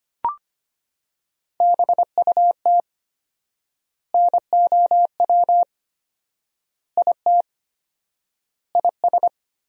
Morse code, heard as a keyed tone, about 25 wpm, 700 Hz. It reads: BUT NOW IT IS
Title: 4x Word-Spacing - Sentences from Top 300 Words - 25wpm